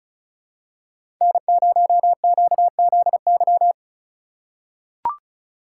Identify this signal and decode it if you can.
Morse code, heard as a keyed tone, about 36 wpm, 700 Hz. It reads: N0QZY
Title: USA Call Signs - 5 Characters - 2x Character Spacing - 35wpm